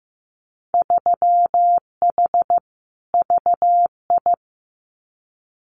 Morse code, heard as a keyed tone, about 15 wpm, 700 Hz. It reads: 3H VI
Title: Winter Field Day Exchanges Encore 15wpm